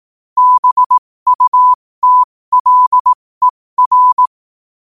Morse code, heard as a keyed tone, about 18 wpm, spaced as for 15 wpm, 1000 Hz. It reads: BUTLER
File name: qotdfiles / 15wpm